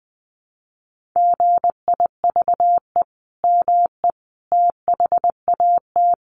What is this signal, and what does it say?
Morse code, heard as a keyed tone, about 20 wpm, 700 Hz. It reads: GIVE ME THAT
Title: Sentences from Top 1500 Words 20wpm